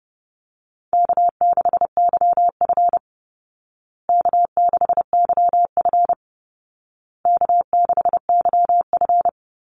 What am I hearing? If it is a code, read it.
Morse code, heard as a keyed tone, about 30 wpm, 700 Hz. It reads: K6YF K6YF K6YF
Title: Easy Call Signs Repeated 3x - 30wpm